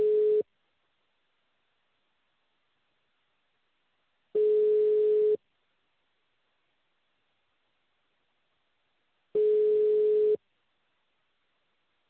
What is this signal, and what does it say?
Ringback tone (Europe, 425 Hz).